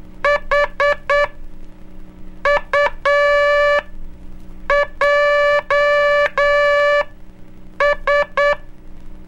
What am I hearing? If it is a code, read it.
Morse code, heard as a keyed tone, about 9 wpm, spaced as for 6 wpm, 580 Hz. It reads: HUJS
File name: cursos / Curso CW mp3 / CW Aula - 04